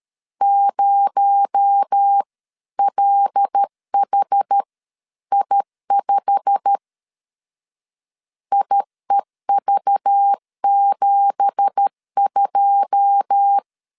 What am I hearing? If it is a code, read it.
Morse code, heard as a keyed tone, about 13 wpm, spaced as for 11 wpm, 800 Hz. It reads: 0LHI5 IEV72
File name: CW / DET / 1 9 1500C S A 12wpm 3800